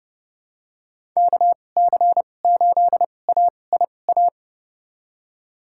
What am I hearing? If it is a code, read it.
Morse code, heard as a keyed tone, about 30 wpm, spaced as for 22 wpm, 700 Hz. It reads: KC8AIA
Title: USA Call Signs - 6 Characters - 2x Character Spacing - 30wpm